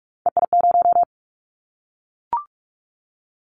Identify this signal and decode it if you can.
Morse code, heard as a keyed tone, about 44 wpm, 700 Hz. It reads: EI0
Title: Callsign Prefixes for Top 50 DXCC - 45wpm